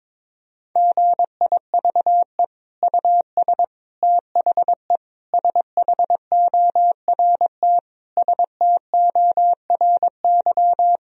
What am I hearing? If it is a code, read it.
Morse code, heard as a keyed tone, about 22 wpm, 700 Hz. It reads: GIVE US THE SHORT STORY
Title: Sentences from Top 1000 Words - Review Part 4 - 22wpm